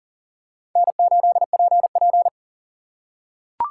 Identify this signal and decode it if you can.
Morse code, heard as a keyed tone, about 40 wpm, 700 Hz. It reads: N8PP
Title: USA Call Signs - 3x Character Spacing - 40wpm